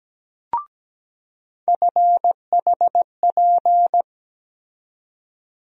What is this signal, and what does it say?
Morse code, heard as a keyed tone, about 17 wpm, 700 Hz. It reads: FHP
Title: USA Call Sign Postfixes - 3 Letters 17wpm